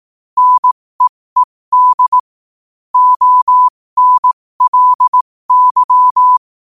Morse code, heard as a keyed tone, about 18 wpm, spaced as for 15 wpm, 1000 Hz. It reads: NEED ONLY
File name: qotdfiles / 15wpm